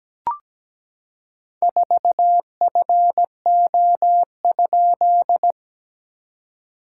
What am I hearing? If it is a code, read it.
Morse code, heard as a keyed tone, about 17 wpm, 690 Hz. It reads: 4FO?
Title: SOTA QSO Practice 17wpm